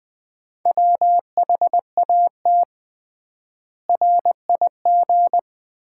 Morse code, heard as a keyed tone, about 20 wpm, 700 Hz. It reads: WHAT RIG
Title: Common QSO Phrases - 3x Word-Spacing - 20wpm